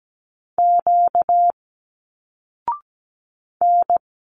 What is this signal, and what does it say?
Morse code, heard as a keyed tone, about 17 wpm, 700 Hz. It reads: Q N
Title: Single Letters 17wpm